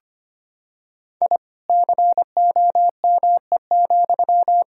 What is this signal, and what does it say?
Morse code, heard as a keyed tone, about 25 wpm, 700 Hz. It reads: I COME,
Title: Sentences from Top 1000 Words - Review Part 5 - 25wpm